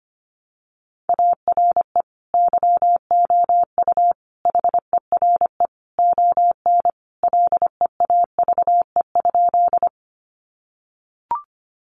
Morse code, heard as a keyed tone, about 25 wpm, 700 Hz. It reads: ARE YOU HERE ON LEAVE?
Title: Sentences from Top 400 Words 25wpm